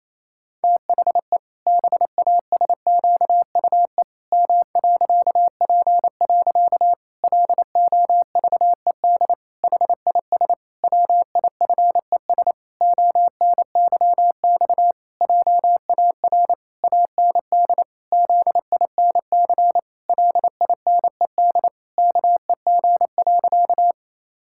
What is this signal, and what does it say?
Morse code, heard as a keyed tone, about 28 wpm, 700 Hz. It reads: THE BASQUE M.P. LOVED HIS WIFES ONYX JAR AND ZINC LINED KEG.
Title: Pangram Challenge v2 - 28wpm